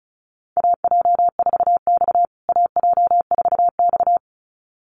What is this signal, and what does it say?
Morse code, heard as a keyed tone, about 34 wpm, 700 Hz. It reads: AJ4X AJ4X
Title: USA Call Signs Repeated 2x 35wpm